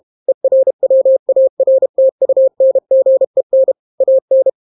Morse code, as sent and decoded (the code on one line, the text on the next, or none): . .-. .-- .- .-. - ..- -. --. -. / .- -.
ERWARTUNGEN AN